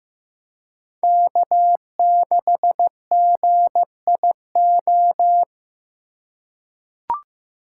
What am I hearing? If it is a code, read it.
Morse code, heard as a keyed tone, about 15 wpm, 700 Hz. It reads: K6GIO
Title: USA Call Signs - 5 Characters - 3x Character Spacing - 15wpm